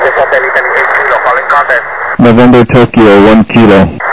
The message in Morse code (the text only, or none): none